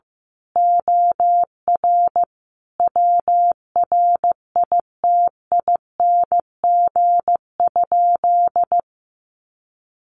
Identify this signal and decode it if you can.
Morse code, heard as a keyed tone, about 15 wpm, 700 Hz. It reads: OR WRITING?
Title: Sentences from Top 1200 Words 15wpm